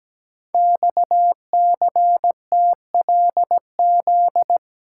Morse code, heard as a keyed tone, about 17 wpm, 700 Hz. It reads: XCTLZ
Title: Code Groups - 5 Letters - 17wpm